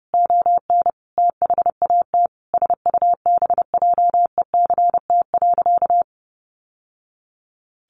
Morse code, heard as a keyed tone, about 30 wpm, 700 Hz. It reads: ON THAT SUBJECT.
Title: Sentences from Top 800 Words 30wpm